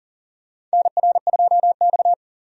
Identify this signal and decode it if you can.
Morse code, heard as a keyed tone, about 40 wpm, 700 Hz. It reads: NR2X
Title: USA Call Signs - 4 Characters - 40wpm